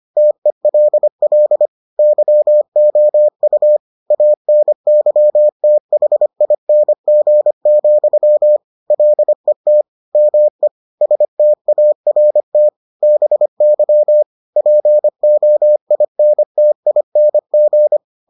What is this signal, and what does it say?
Morse code, heard as a keyed tone, about 25 wpm, 600 Hz. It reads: TELL YOU ANYTHING, LET ME START BY POINTING